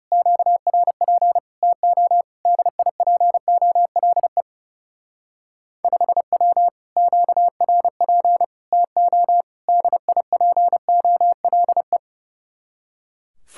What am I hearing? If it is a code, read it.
Morse code, heard as a keyed tone, about 38 wpm, 700 Hz. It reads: QRP TO DIPOLE 5W QRP TO DIPOLE